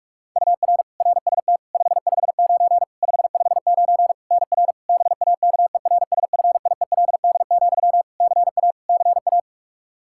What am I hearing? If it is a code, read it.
Morse code, heard as a keyed tone, about 47 wpm, 700 Hz. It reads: UR RST 559 559 NR BAKERSFIELD, CA CA
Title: Practice QSO - Exchange 1 - Beginning - 45wpm